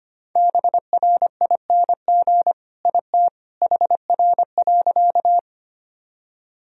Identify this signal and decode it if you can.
Morse code, heard as a keyed tone, about 25 wpm, 700 Hz. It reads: BRING IT HR.
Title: Sentences from Top 500 Words and Abbreviations 25wpm